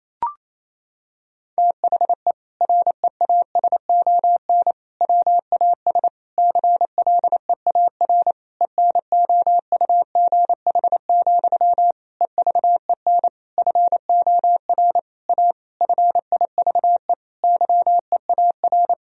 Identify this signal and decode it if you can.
Morse code, heard as a keyed tone, about 28 wpm, 700 Hz. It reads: THE REASON WAS CLEAR ENOUGH, EVEN FOR A FIVE YEAR
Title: Sentences from Top 1000 Words - Review Part 2 - 28wpm